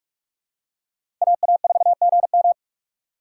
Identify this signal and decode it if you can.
Morse code, heard as a keyed tone, about 44 wpm, 700 Hz. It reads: AA4GM
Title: USA Call Signs - 4x Character Spacing - 45wpm